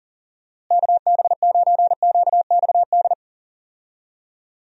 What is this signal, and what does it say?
Morse code, heard as a keyed tone, about 40 wpm, 700 Hz. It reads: KB9QXD
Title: USA Call Signs - 6 Characters - Repeated 3x - 40wpm